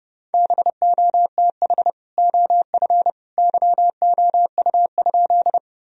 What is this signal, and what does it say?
Morse code, heard as a keyed tone, about 32 wpm, 700 Hz. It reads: BOTH OF YOU?
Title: Sentences from Top 1000 Words - Review Part 1 - 30wpm